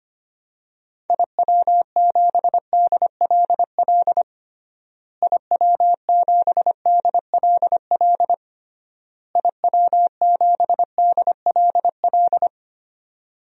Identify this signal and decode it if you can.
Morse code, heard as a keyed tone, about 25 wpm, 700 Hz. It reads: IW7DLL IW7DLL IW7DLL
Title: DXCC-100 Call Signs - 6 Characters - Repeated 3x - 25wpm